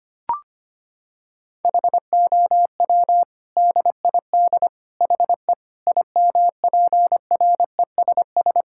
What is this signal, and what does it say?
Morse code, heard as a keyed tone, about 25 wpm, 700 Hz. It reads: HOW DID HE IMPRESS
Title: Sentences from Top 1600 Words 25wpm